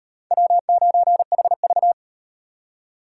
Morse code, heard as a keyed tone, about 37 wpm, 700 Hz. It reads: W9HV